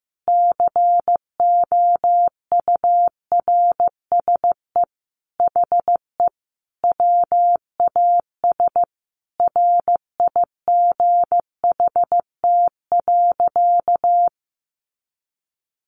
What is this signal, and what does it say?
Morse code, heard as a keyed tone, about 14 wpm, 700 Hz. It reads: COURSE HE WAS RIGHT.
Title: Sentences from Top 500 Words 15wpm